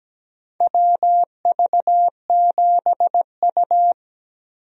Morse code, heard as a keyed tone, about 17 wpm, 700 Hz. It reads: WV7U
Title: USA Call Signs - 4 Characters - Repeated 3x - 17wpm